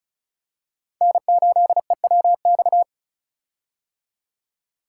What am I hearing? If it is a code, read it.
Morse code, heard as a keyed tone, about 36 wpm, 700 Hz. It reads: N8EWX
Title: K1USN SST QSO Practice - 35wpm